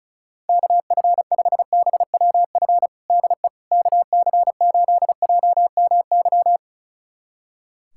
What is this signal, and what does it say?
Morse code, heard as a keyed tone, about 34 wpm, 700 Hz. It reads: KF5BWF DE KC8JMY
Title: USA Callsign DE Callsign - 6 Characters - 35wpm